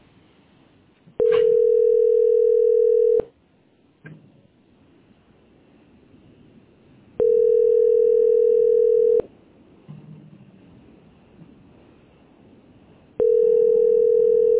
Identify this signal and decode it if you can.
Ringback tone (North America), 440 + 480 Hz.
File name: ca2789ce-fdfb-4e8e-8739